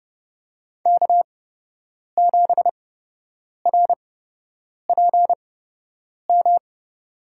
Morse code, heard as a keyed tone, about 30 wpm, spaced as for 8 wpm, 700 Hz. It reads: K7RPM
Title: USA Call Signs - 5 Characters - 8x Character Spacing - 30wpm